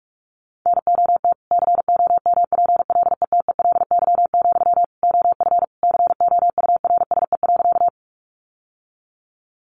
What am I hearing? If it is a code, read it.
Morse code, heard as a keyed tone, about 46 wpm, 700 Hz. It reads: NOT COMPLETELY, OF COURSE.
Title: Sentences from Top 1100 Words 45wpm